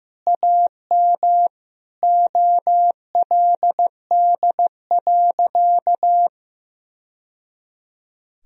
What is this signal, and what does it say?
Morse code, heard as a keyed tone, about 14 wpm, 700 Hz. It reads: AM OLD.
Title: Sentences from Top 1000 Words - Review Part 3 - 15wpm